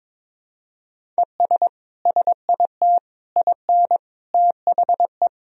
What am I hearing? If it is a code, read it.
Morse code, heard as a keyed tone, about 22 wpm, 700 Hz. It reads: ES SIT IN THE
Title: Sentences from Top 500 Words and Abbreviations 22wpm